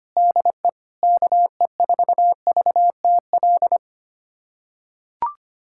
Morse code, heard as a keyed tone, about 25 wpm, 700 Hz. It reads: DE KE4VTL
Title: USA Callsign DE Callsign - 6 Characters - 25wpm